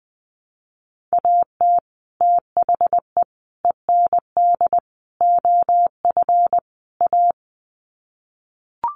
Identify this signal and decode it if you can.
Morse code, heard as a keyed tone, about 20 wpm, 700 Hz. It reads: AT THE END OF A
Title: Sets of 5 Words 20wpm